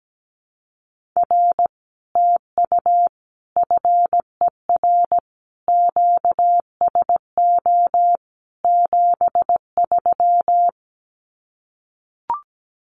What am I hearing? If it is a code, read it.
Morse code, heard as a keyed tone, about 17 wpm, 700 Hz. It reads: R TU FER QSO 73
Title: Common QSO Phrases Encore 17wpm